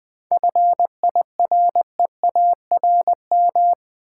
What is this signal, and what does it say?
Morse code, heard as a keyed tone, about 20 wpm, 700 Hz. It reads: FIREARM
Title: 7 Letter Words - First 3 Letters Given - 20wpm